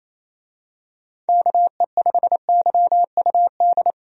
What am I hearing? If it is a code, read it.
Morse code, heard as a keyed tone, about 28 wpm, 700 Hz. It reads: KE5YUD
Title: USA Call Signs - 6 Characters - Repeated 2x - 28wpm